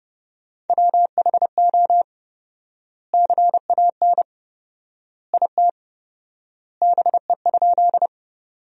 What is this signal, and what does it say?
Morse code, heard as a keyed tone, about 30 wpm, 700 Hz. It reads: WHO CAN IT BE?
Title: 4x Word-Spacing - Sentences from Top 100 Words - 30wpm